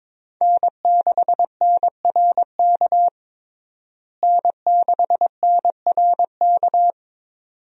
Morse code, heard as a keyed tone, about 21 wpm, 700 Hz. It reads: N6NRK N6NRK
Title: USA Call Signs - 5 Characters - Repeated 2x - 22wpm